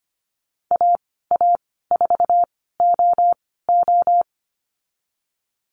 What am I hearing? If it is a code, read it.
Morse code, heard as a keyed tone, about 25 wpm, spaced as for 16 wpm, 700 Hz. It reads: AA4OO
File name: Lesson 15g - Number 5 - Call Signs 25wpm